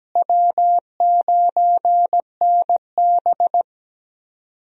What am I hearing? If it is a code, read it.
Morse code, heard as a keyed tone, about 17 wpm, 700 Hz. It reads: W9NB
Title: USA Call Signs - 4 Characters - Repeated 2x - 17wpm